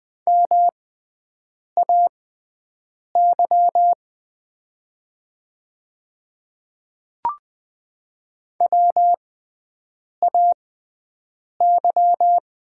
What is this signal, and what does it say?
Morse code, heard as a keyed tone, about 20 wpm, spaced as for 7 wpm, 700 Hz. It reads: MAY WAY
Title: 6x Character Spacing - Top 100 Words - 20wpm